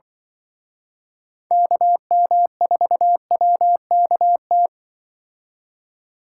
Morse code, heard as a keyed tone, about 24 wpm, 700 Hz. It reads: KM4WKT
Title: Speed Racing - USA Call Signs 20wpm